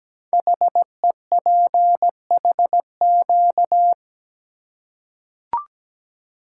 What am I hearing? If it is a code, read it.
Morse code, heard as a keyed tone, about 17 wpm, 700 Hz. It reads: HEPHQ
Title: Code Groups - 5 Letters - 17wpm